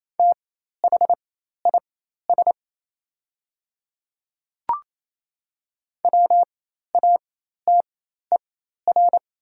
Morse code, heard as a keyed tone, about 28 wpm, spaced as for 13 wpm, 700 Hz. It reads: THIS WATER
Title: 4x Character Spacing - Top 100 Words - 28wpm